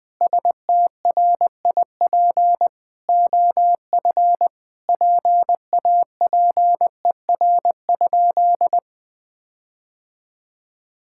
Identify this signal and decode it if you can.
Morse code, heard as a keyed tone, about 20 wpm, 700 Hz. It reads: STRIP OF PAPER?
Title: Sentences from Top 1600 Words 20wpm